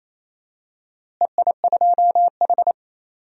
Morse code, heard as a keyed tone, about 28 wpm, 700 Hz. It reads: EI2H
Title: DXCC-100 Call Signs - 4 Characters - Repeated 3x - 28wpm